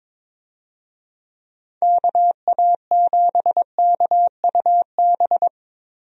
Morse code, heard as a keyed tone, about 22 wpm, 700 Hz. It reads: KA7KUB
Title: USA Call Signs - 6 Characters - Repeated 2x - 22wpm